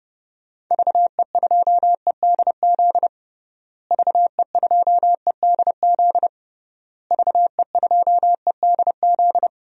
Morse code, heard as a keyed tone, about 30 wpm, 700 Hz. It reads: VE2EDZ VE2EDZ VE2EDZ
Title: DXCC-100 Call Signs - 6 Characters - Repeated 3x - 30wpm